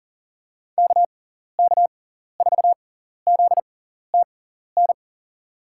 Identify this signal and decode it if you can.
Morse code, heard as a keyed tone, about 37 wpm, spaced as for 14 wpm, 700 Hz. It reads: KK4ZTN
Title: USA Call Signs - 6 Characters - 6x Character Spacing - 40wpm